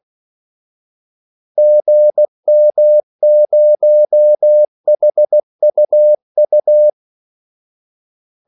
Morse code, heard as a keyed tone, about 16 wpm, 600 Hz. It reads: GM0HUU
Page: Kaldesignaler 16 wpm